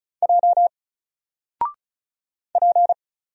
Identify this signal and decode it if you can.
Morse code, heard as a keyed tone, about 36 wpm, 700 Hz. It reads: J P
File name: Single-Letter-Number-35wpm